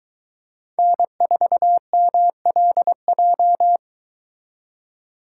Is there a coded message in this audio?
Yes: Morse code from 0.8 to 3.8 s, heard as a keyed tone, about 22 wpm, 700 Hz. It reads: N4MLJ